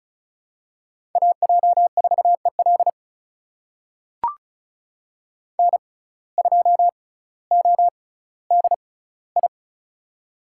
Morse code, heard as a keyed tone, about 36 wpm, spaced as for 18 wpm, 700 Hz. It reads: AJ4EL N2ODI
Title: USA Call Signs - 6x Character Spacing - 35wpm